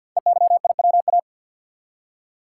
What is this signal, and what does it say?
Morse code, heard as a keyed tone, about 46 wpm, 700 Hz. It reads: EYIWA
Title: Code Groups - 5 Letters - 50wpm